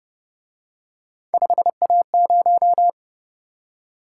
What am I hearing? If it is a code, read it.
Morse code, heard as a keyed tone, about 30 wpm, 700 Hz. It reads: 5A0